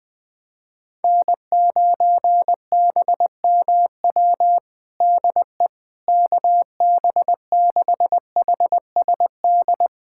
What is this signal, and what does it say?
Morse code, heard as a keyed tone, about 21 wpm, 700 Hz. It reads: N9BMW DE KB6HSD
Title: USA Callsign DE Callsign 20wpm